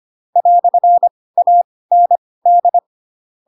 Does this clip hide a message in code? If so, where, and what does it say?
Yes: Morse code from 0.4 to 2.8 s, heard as a keyed tone, about 25 wpm, spaced as for 18 wpm, 700 Hz. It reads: "AND